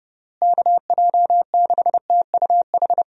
Morse code, heard as a keyed tone, about 32 wpm, 700 Hz. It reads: KJ6TUH
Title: USA Call Signs - 6 Characters - Repeated 3x - 30wpm